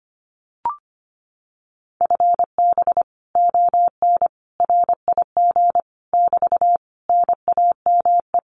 Morse code, heard as a keyed tone, about 25 wpm, 700 Hz. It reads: FB ON RIG <BT> NAME
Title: Common QSO Phrases Encore 25wpm